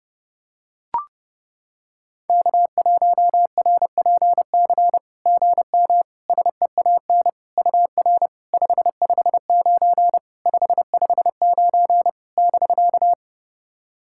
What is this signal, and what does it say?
Morse code, heard as a keyed tone, about 30 wpm, 700 Hz. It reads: K1RPC GM SEAN UR 559 559 <BK>
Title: SOTA QSO Practice 30wpm